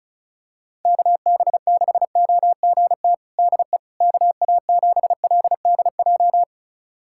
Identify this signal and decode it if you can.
Morse code, heard as a keyed tone, about 36 wpm, 700 Hz. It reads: KB6OGT DE KA7LDJ
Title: USA Callsign DE Callsign - 6 Characters - 35wpm